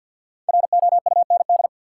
Morse code, heard as a keyed tone, about 53 wpm, 700 Hz. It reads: ROUND
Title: Sets of 5 Words Encore - New Word Drill - 50wpm